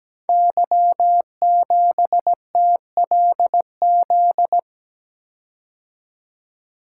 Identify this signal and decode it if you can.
Morse code, heard as a keyed tone, about 17 wpm, 700 Hz. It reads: Y7TLZ
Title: Code Groups - 5 Letter:Numbers - 17wpm